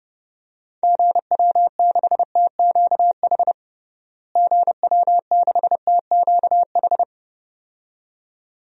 Morse code, heard as a keyed tone, about 30 wpm, 700 Hz. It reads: GW6TQH GW6TQH
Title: DXCC-100 Call Signs - 6 Characters - Repeated 2x - 30wpm